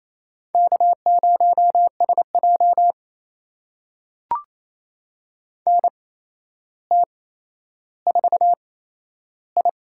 Morse code, heard as a keyed tone, about 28 wpm, spaced as for 12 wpm, 700 Hz. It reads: K0SJ NT4I
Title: USA Call Signs - 4 Characters - 8x Character Spacing - 28wpm